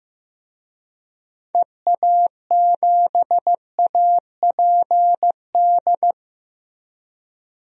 Morse code, heard as a keyed tone, about 16 wpm, 700 Hz. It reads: EA7APD
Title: DXCC-100 Call Signs - 6 Characters - 15wpm